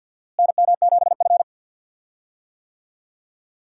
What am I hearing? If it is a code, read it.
Morse code, heard as a keyed tone, about 52 wpm, 700 Hz. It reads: NM7F
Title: USA Call Signs - 4 Characters - Repeated 3x - 50wpm